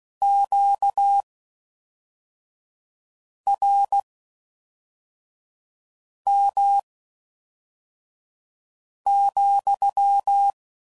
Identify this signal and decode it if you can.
Morse code, heard as a keyed tone, about 16 wpm, spaced as for 4 wpm, 800 Hz. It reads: QRM,